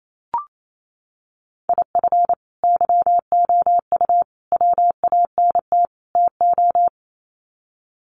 Morse code, heard as a keyed tone, about 28 wpm, 700 Hz. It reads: IF YOU WANT TO
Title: Sets of 4 Words 28wpm